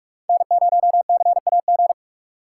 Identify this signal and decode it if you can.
Morse code, heard as a keyed tone, about 47 wpm, 700 Hz. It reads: N0CAG
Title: USA Call Signs - 5 Characters - 3x Character Spacing - 45wpm